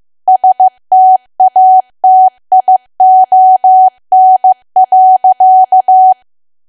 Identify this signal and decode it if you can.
Morse code, heard as a keyed tone, about 15 wpm, 750 Hz. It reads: STATION.
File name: cwa / W1AW-Level2 / 15WPM